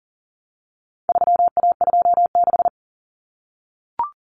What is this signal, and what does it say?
Morse code, heard as a keyed tone, about 40 wpm, 700 Hz. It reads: 3A26
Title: Common Cut Numbers - 4-Digit 40wpm